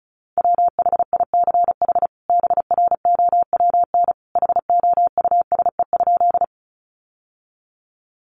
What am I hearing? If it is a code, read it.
Morse code, heard as a keyed tone, about 34 wpm, 700 Hz. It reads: WHICH BROWN HOUSE?
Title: Sentences from Top 1200 Words 35wpm